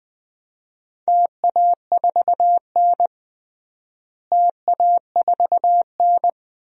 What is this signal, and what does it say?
Morse code, heard as a keyed tone, about 20 wpm, 700 Hz. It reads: TA4N TA4N
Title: DXCC-100 Call Signs - 4 Characters - Repeated 3x - 20wpm